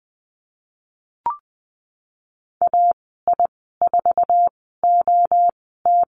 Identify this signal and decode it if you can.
Morse code, heard as a keyed tone, about 20 wpm, spaced as for 14 wpm, 700 Hz. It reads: AI4OT
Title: 088 Lesson 15g - Number 5 - Call Signs 20wpm